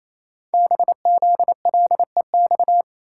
Morse code, heard as a keyed tone, about 27 wpm, 700 Hz. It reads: BZLEX
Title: Code Groups - 5 Letters - 28wpm